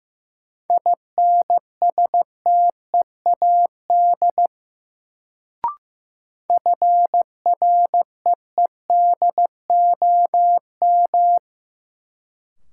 Morse code, heard as a keyed tone, about 15 wpm, 700 Hz. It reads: INSTEAD FREEDOM
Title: Sentences from Top 800 Words - New Word Drill - 15wpm